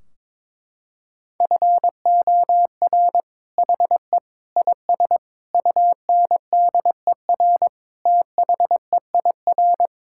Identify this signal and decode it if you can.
Morse code, heard as a keyed tone, about 22 wpm, 700 Hz. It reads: FOR HE IS UNDER THEIR
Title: Sentences from Top 1600 Words 22wpm